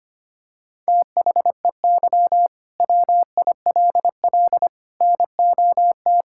Morse code, heard as a keyed tone, about 25 wpm, 700 Hz. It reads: THEY WILL NOT
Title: Sentences from Top 1000 Words - Review Part 5 - 25wpm